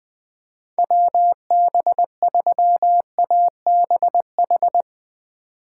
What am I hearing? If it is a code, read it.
Morse code, heard as a keyed tone, about 20 wpm, 700 Hz. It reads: WB3ABH
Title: USA Call Signs - 6 Characters - 8x Character Spacing - 20wpm